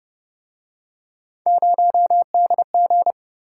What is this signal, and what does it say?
Morse code, heard as a keyed tone, about 32 wpm, 700 Hz. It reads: 0DG